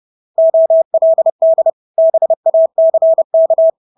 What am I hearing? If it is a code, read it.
Morse code, heard as a keyed tone, about 30 wpm, 650 Hz. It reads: OLD BACK
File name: Engelske ord 30-30 wpm